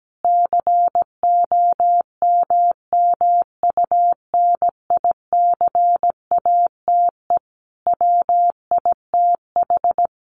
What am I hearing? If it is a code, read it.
Morse code, heard as a keyed tone, about 17 wpm, 700 Hz. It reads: COMMUNICATE WITH
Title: Sentences from Top 1300 Words 17wpm